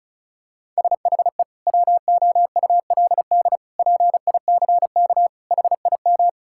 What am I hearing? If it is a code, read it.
Morse code, heard as a keyed tone, about 34 wpm, 700 Hz. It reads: SHE WOULD PICK HIM
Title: Sentences from Top 1600 Words 35wpm